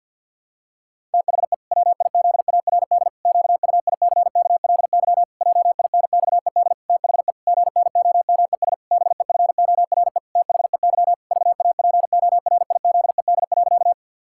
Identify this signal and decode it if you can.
Morse code, heard as a keyed tone, about 50 wpm, 700 Hz. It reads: THE WIZARD QUICKLY JINXED THE GNOMES BEFORE THEY VAPORIZED.
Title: Pangram Challenge v2 - 50wpm